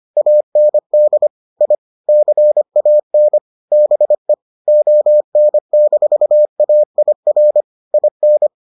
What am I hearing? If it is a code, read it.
Morse code, heard as a keyed tone, about 25 wpm, 600 Hz. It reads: AND I CAN BE ON-AIR IN